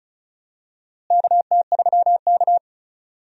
Morse code, heard as a keyed tone, about 34 wpm, 700 Hz. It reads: KT3K
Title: USA Call Signs - 4 Characters - Repeated 2x - 35wpm